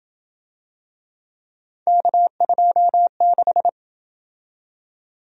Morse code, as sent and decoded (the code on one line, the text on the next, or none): -.- ..--- -....
K26